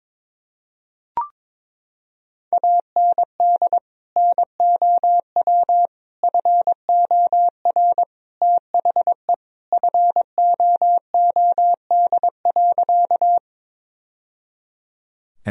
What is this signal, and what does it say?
Morse code, heard as a keyed tone, about 22 wpm, 700 Hz. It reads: AND NOW FOR THE FOOD.
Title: Sentences from Top 1000 Words - Review Part 1 - 22wpm